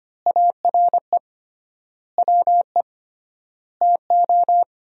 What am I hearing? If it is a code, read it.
Morse code, heard as a keyed tone, about 25 wpm, 700 Hz. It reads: ARE WE TO
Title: 3x Word-Spacing - Sentences from Top 200 Words - 25wpm